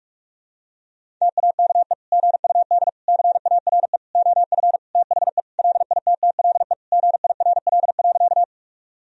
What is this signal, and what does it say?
Morse code, heard as a keyed tone, about 46 wpm, 700 Hz. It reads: TAKE GUD CARE OF THE LITTLE GIRL.
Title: Sentences from Top 400 Words and Abbreviations 45wpm